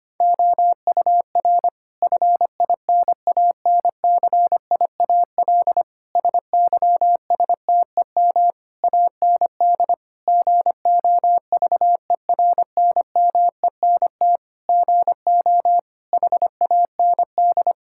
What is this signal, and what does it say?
Morse code, heard as a keyed tone, about 25 wpm, 700 Hz. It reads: OUR FINANCIAL SYSTEM AND GOVERNMENT GO HAND
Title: Sentences from Top 1000 Words - Review Part 5 - 25wpm